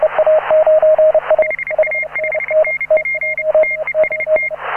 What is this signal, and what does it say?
Morse code, heard as a keyed tone, about 29 wpm, 2100 Hz, with radio noise. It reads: HA5NP TEST
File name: AUDIO2018 / HA9A CW